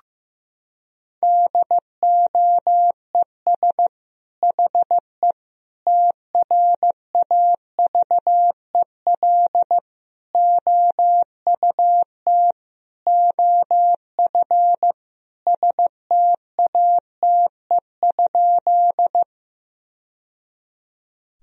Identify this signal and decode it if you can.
Morse code, heard as a keyed tone, about 15 wpm, 700 Hz. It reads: DOES HE TRAVEL OUT OF STATE?
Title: Sentences from Top 1000 Words - Review Part 2 - 15wpm